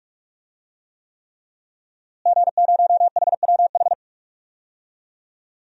Morse code, heard as a keyed tone, about 43 wpm, 700 Hz. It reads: G0HWH
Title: DXCC-100 Call Signs - 5 Characters - Repeated 3x - 45wpm